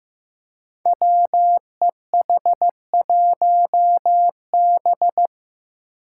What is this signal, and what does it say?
Morse code, heard as a keyed tone, about 15 wpm, 700 Hz. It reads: WEH1B
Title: Five Challenging Letter-Number Combinations 15wpm